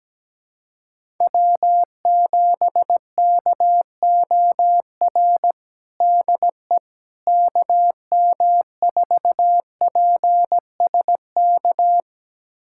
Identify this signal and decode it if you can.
Morse code, heard as a keyed tone, about 17 wpm, 700 Hz. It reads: W7KOR DE KM4PSK
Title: USA Callsign DE Callsign 17wpm